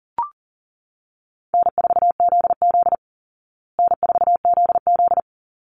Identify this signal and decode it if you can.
Morse code, heard as a keyed tone, about 40 wpm, 700 Hz. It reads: N4ZZ N4ZZ
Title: Easy Call Signs Repeated 2x - 40wpm